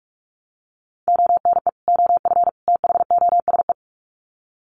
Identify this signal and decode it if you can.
Morse code, heard as a keyed tone, about 44 wpm, 700 Hz. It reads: ONE OF THOSE
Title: Sets of 3 Words 45wpm